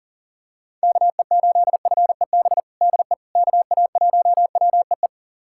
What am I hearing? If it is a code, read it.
Morse code, heard as a keyed tone, about 40 wpm, 700 Hz. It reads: KE8FEB DE KA1WEE
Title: USA Callsign DE Callsign - 6 Characters - 40wpm